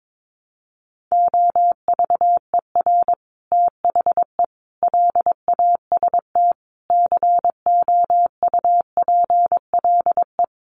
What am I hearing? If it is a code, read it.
Morse code, heard as a keyed tone, about 22 wpm, 700 Hz. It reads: OVER THE LAST COUPLE
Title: Sets of 5 Words Encore 22wpm